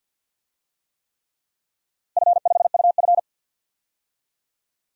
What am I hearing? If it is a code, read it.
Morse code, heard as a keyed tone, about 45 wpm, 700 Hz. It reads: F5UF
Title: DXCC-100 Call Signs - 4 Characters - 50wpm